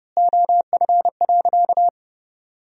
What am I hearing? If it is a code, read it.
Morse code, heard as a keyed tone, about 32 wpm, 700 Hz. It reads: OF.
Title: 3x Word-Spacing - Sentences from Top 300 Words - 30wpm